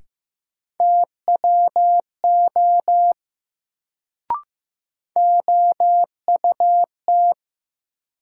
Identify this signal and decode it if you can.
Morse code, heard as a keyed tone, about 15 wpm, 700 Hz. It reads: TWO OUT
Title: Sentences from Top 100 Words - New Word Drill - 15wpm